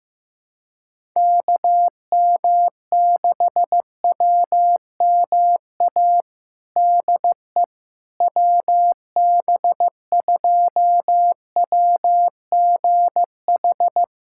Morse code, heard as a keyed tone, about 16 wpm, 700 Hz. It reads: KM6WMA DE WB2WGH
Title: USA Callsign DE Callsign - 6 Characters - 15wpm